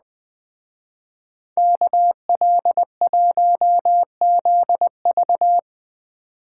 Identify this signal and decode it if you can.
Morse code, heard as a keyed tone, about 19 wpm, 700 Hz. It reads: KL1ZV
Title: Speed Racing - USA 2x2 Call Signs 17wpm